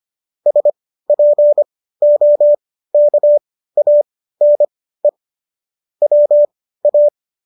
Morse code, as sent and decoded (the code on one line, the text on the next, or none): ... .--. --- -.- .- -. / .-- .-
SPOKANE WA